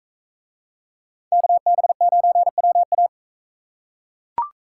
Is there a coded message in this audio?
Yes: Morse code from 1.3 to 3.1 s, heard as a keyed tone, about 41 wpm, 700 Hz. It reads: KB9WA